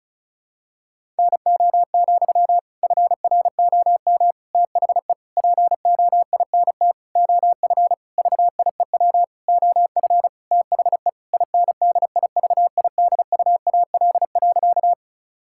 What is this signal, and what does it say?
Morse code, heard as a keyed tone, about 36 wpm, 700 Hz. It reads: NO, FROM THE POINT OF VIEW OF THE INDIVIDUAL.
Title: Sentences from Top 1000 Words - Review Part 6 - 35wpm